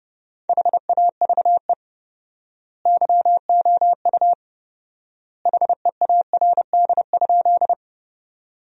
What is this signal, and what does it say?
Morse code, heard as a keyed tone, about 30 wpm, 700 Hz. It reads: HAVE YOU HEARD?
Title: 4x Word-Spacing - Sentences from Top 400 Words - 30wpm